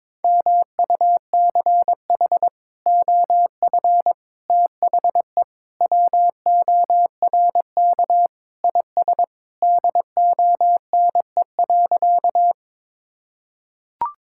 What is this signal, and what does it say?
Morse code, heard as a keyed tone, about 22 wpm, 700 Hz. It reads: MUCH OF THE WORK IS DONE.
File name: practice / 22wpm / Sentences with 3-Word Phrases - 22wpm